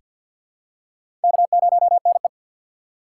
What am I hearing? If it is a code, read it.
Morse code, heard as a keyed tone, about 48 wpm, 690 Hz. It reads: K0NE
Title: USA Call Signs - 4 Characters - 6x Character Spacing - 50wpm